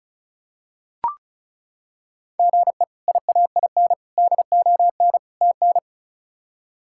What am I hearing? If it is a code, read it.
Morse code, heard as a keyed tone, about 36 wpm, 700 Hz. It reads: GE IAIN DON TN
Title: K1USN SST QSO Practice - 35wpm